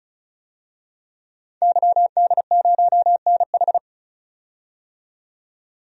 Morse code, heard as a keyed tone, about 34 wpm, 700 Hz. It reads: YD0NH
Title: DXCC-100 Call Signs - 5 Characters - 35wpm